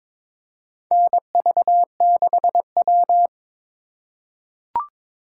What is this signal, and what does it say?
Morse code, heard as a keyed tone, about 22 wpm, 700 Hz. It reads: NV6W
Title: USA Call Signs - 4 Characters - 2x Character Spacing - 22wpm